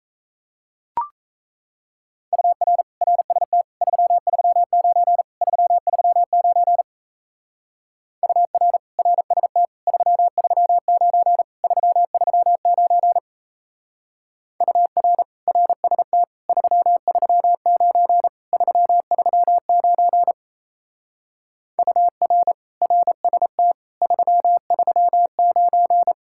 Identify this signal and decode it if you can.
Morse code, heard as a keyed tone, about 33 wpm, 700 Hz. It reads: UR RST 339 339 UR RST 339 339 UR RST 339 339 UR RST 339
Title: Speed-Racing Common QSO Phrases - 28wpm